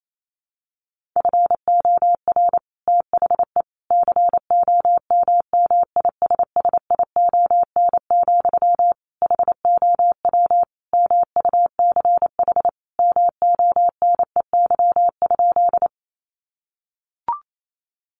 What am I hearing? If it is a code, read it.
Morse code, heard as a keyed tone, about 28 wpm, 700 Hz. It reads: FOR THE COMMISSION, HOW MUCH MONEY?
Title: Sentences from Top 1400 Words 28wpm